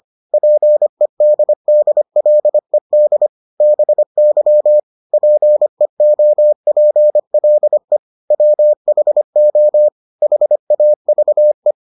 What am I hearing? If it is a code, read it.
Morse code, heard as a keyed tone, about 25 wpm, 600 Hz. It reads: PEDDLED BY PEOPLE WHO HAVE